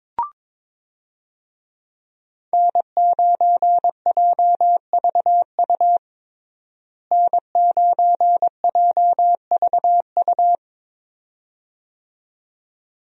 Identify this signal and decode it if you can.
Morse code, heard as a keyed tone, about 22 wpm, 700 Hz. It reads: N9JVU N9JVU
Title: USA Call Signs - 5 Characters - Repeated 2x - 22wpm